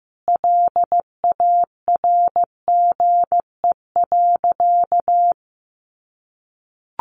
Morse code, heard as a keyed tone, about 15 wpm, 700 Hz. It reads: LARGE.
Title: Sentences from Top 1200 Words 15wpm